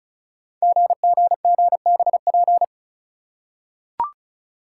Morse code, heard as a keyed tone, about 36 wpm, 690 Hz. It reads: GGGBP
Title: Code Groups - 5 Letters - 35wpm